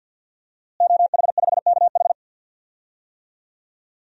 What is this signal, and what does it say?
Morse code, heard as a keyed tone, about 50 wpm, 700 Hz. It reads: OH5KH
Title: DXCC-100 Call Signs - 5 Characters - Repeated 2x - 50wpm